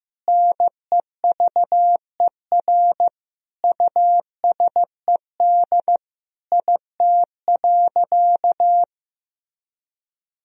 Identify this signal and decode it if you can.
Morse code, heard as a keyed tone, about 15 wpm, 700 Hz. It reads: NEVER USED IT.
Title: Sentences from Top 1000 Words - Review Part 2 - 15wpm